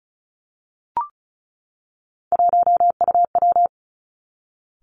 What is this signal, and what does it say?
Morse code, heard as a keyed tone, about 36 wpm, 700 Hz. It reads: 1UW
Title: Three Challenging Letter-Number Combinations 35wpm